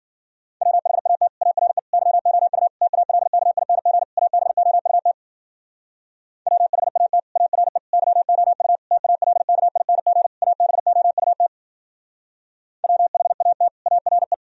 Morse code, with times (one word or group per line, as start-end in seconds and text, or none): 0.6-1.3 WHAT
1.4-1.8 ARE
1.9-2.7 YOU
2.8-4.0 TALKING
4.2-5.1 ABOUT
6.5-7.2 WHAT
7.4-7.8 ARE
7.9-8.8 YOU
8.9-10.3 TALKING
10.4-11.5 ABOUT
12.8-13.7 WHAT
13.9-14.3 ARE